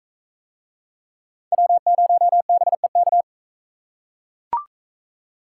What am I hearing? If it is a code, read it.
Morse code, heard as a keyed tone, about 43 wpm, 700 Hz. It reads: W0BEK